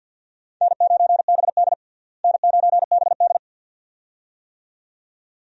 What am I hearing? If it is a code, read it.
Morse code, heard as a keyed tone, about 51 wpm, 700 Hz. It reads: N9BD N9BD
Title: USA Call Signs - 4 Characters - Repeated 2x - 50wpm